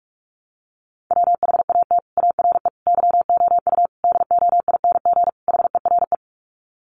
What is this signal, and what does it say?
Morse code, heard as a keyed tone, about 44 wpm, 700 Hz. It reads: WHAT ARE YOU DOING HERE
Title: Sets of 5 Words 45wpm